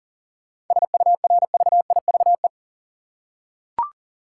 Morse code, heard as a keyed tone, about 40 wpm, 700 Hz. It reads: SURVIVE
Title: Seven Letter Words - 40wpm